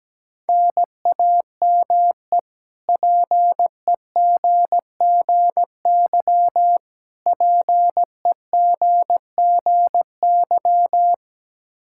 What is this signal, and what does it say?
Morse code, heard as a keyed tone, about 17 wpm, 700 Hz. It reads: NAME PEGGY PEGGY
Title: K1USN SST QSO Practice 17wpm